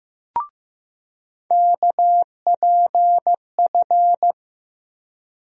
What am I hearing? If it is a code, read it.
Morse code, heard as a keyed tone, about 15 wpm, 700 Hz. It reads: KPF
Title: USA Call Sign Postfixes - 3 Letters 15wpm